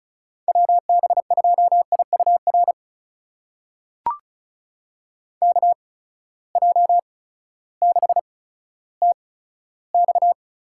Morse code, heard as a keyed tone, about 36 wpm, spaced as for 16 wpm, 700 Hz. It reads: WB2IUR KJ6TX
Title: USA Call Signs - 8x Character Spacing - 35wpm